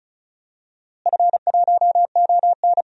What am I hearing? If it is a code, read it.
Morse code, heard as a keyed tone, about 36 wpm, 700 Hz. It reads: F1ON
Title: DXCC-100 Call Signs - 4 Characters - Repeated 3x - 35wpm